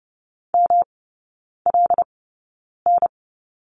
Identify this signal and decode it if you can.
Morse code, heard as a keyed tone, about 30 wpm, 700 Hz. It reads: M L N
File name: Lesson 19e - Letter w - Three Characters 30wpm